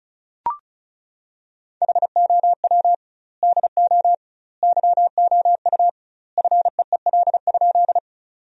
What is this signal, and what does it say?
Morse code, heard as a keyed tone, about 36 wpm, 700 Hz. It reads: HOW DO YOU FEEL?
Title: 2x Word-Spacing - Sentences from Top 400 Words - 35wpm